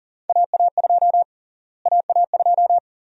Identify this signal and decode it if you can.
Morse code, heard as a keyed tone, about 40 wpm, 700 Hz. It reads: AA2 AA2
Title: USA Call Sign Prefixes - Repeated 2x - 40wpm